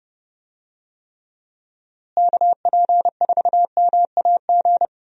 Morse code, heard as a keyed tone, about 31 wpm, 700 Hz. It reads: KP4MAG
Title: USA Call Signs - 6 Characters - Repeated 3x - 30wpm